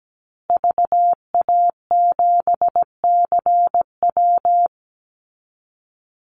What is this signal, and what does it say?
Morse code, heard as a keyed tone, about 17 wpm, 700 Hz. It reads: VA7CW
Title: Call Signs Repeated 3x - 17wpm